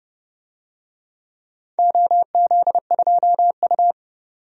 Morse code, heard as a keyed tone, about 32 wpm, 700 Hz. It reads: OZ2U
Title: DXCC-100 Call Signs - 4 Characters - 30wpm